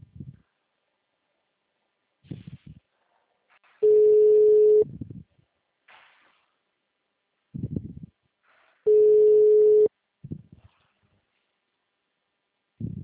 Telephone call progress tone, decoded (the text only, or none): ringback tone